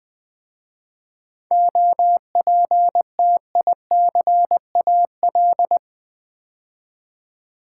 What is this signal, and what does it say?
Morse code, heard as a keyed tone, about 20 wpm, 700 Hz. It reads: OPTICAL
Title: 7 Letter Words - First 3 Letters Given - 20wpm